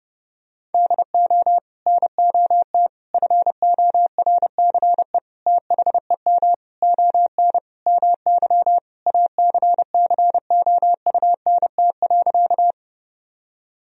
Morse code, heard as a keyed tone, about 30 wpm, 700 Hz. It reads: DO NOT FORCE THEM ON MY ACCOUNT.
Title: Sentences from Top 1000 Words - Review Part 3 - 30wpm